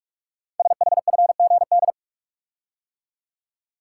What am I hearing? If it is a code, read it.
Morse code, heard as a keyed tone, about 44 wpm, 700 Hz. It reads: SHFGD
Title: Code Groups - 5 Letters - 45wpm